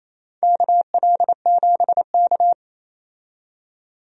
Morse code, heard as a keyed tone, about 27 wpm, 690 Hz. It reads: KL7K
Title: DXCC-100 Call Signs - 4 Characters - 28wpm